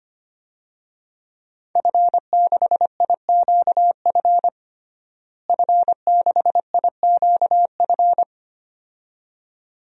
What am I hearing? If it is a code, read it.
Morse code, heard as a keyed tone, about 25 wpm, 700 Hz. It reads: F6IQF F6IQF
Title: DXCC-100 Call Signs - 5 Characters - Repeated 2x - 25wpm